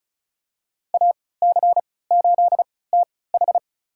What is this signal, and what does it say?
Morse code, heard as a keyed tone, about 36 wpm, spaced as for 20 wpm, 700 Hz. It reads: AC8TH
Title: USA Call Signs - 5 Characters - 3x Character Spacing - 35wpm